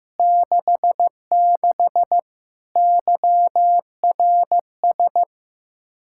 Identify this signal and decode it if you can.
Morse code, heard as a keyed tone, about 15 wpm, 700 Hz. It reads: 66 YRS
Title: Practice QSO - Exchange 3 - More Info - 15wpm